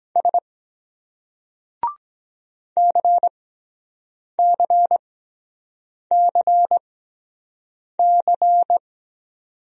Morse code, heard as a keyed tone, about 23 wpm, spaced as for 7 wpm, 700 Hz. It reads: S CCCC